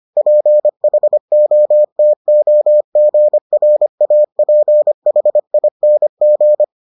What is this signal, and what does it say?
Morse code, heard as a keyed tone, about 25 wpm, 600 Hz. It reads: PHOTOGRAPHING